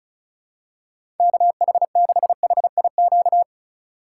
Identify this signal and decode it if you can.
Morse code, heard as a keyed tone, about 36 wpm, 700 Hz. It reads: KH6HIQ
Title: USA Call Signs - 4x Character Spacing - 35wpm